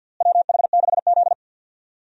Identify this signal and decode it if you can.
Morse code, heard as a keyed tone, about 51 wpm, 700 Hz. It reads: WH6Z
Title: USA Call Signs - 4 Characters - Repeated 2x - 50wpm